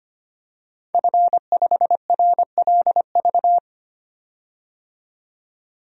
Morse code, heard as a keyed tone, about 25 wpm, 700 Hz. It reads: F5RLV
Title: DXCC-100 Call Signs - 5 Characters - 25wpm